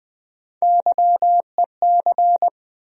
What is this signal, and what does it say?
Morse code, heard as a keyed tone, about 20 wpm, 700 Hz. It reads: YEC
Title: USA Call Sign Postfixes - 3 Letters 20wpm